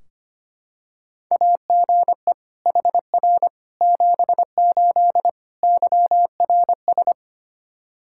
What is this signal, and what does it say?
Morse code, heard as a keyed tone, about 25 wpm, 700 Hz. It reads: AGE HR 78 YRS
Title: K1USN SST QSO Practice 25wpm